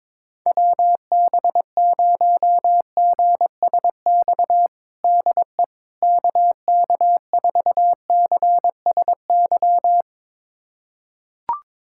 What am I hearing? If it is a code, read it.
Morse code, heard as a keyed tone, about 22 wpm, 700 Hz. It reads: WB0GSX DE KK4CSY
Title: USA Callsign DE Callsign - 6 Characters - 22wpm